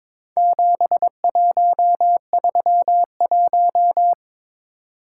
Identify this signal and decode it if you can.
Morse code, heard as a keyed tone, about 22 wpm, 700 Hz. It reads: 7131
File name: practice / 22wpm / Four Letter-Number Combinations - 22wpm